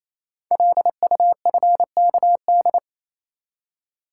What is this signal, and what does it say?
Morse code, heard as a keyed tone, about 29 wpm, 700 Hz. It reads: LUFKD
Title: Code Groups - 5 Letters - 28wpm